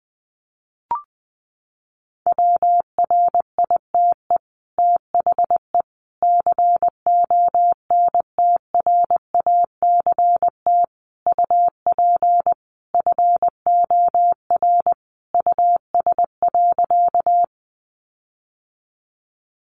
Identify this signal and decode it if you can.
Morse code, heard as a keyed tone, about 20 wpm, 700 Hz. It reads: WRITE THE CONTRACT UP FOR US.
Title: Sentences from Top 1400 Words 20wpm